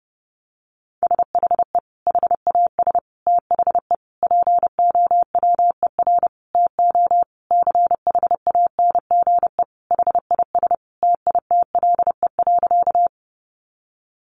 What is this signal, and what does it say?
Morse code, heard as a keyed tone, about 30 wpm, 700 Hz. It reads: SHE HAS THE POWER TO CHANGE HIS TITLE.
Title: Sentences from Top 1200 Words 30wpm